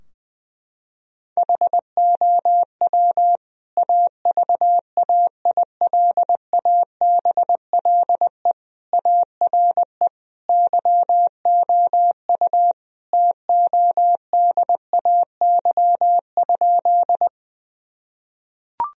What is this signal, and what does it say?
Morse code, heard as a keyed tone, about 21 wpm, 700 Hz. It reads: HOW AVAILABLE ARE YOU TODAY?
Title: Sentences from Top 1000 Words - Review Part 3 - 20wpm